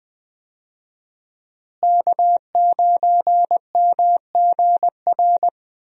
Morse code, heard as a keyed tone, about 20 wpm, 700 Hz. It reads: K9MGR
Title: USA Call Signs - 5 Characters - Repeated 3x - 20wpm